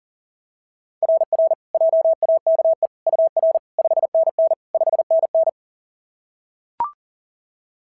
Morse code, heard as a keyed tone, about 37 wpm, 690 Hz. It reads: RR JAKE UR 5NN 5NN
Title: SOTA QSO Practice 40wpm